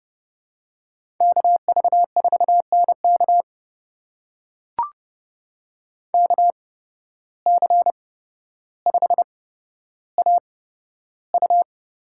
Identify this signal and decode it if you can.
Morse code, heard as a keyed tone, about 30 wpm, spaced as for 13 wpm, 700 Hz. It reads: KV4NK KC5AU
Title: USA Call Signs - 5 Characters - 8x Character Spacing - 30wpm